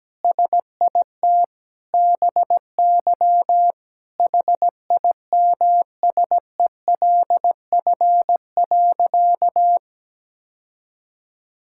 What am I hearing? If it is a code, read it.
Morse code, heard as a keyed tone, about 17 wpm, 700 Hz. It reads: SIT BY HIMSELF.
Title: Sentences from Top 1600 Words 17wpm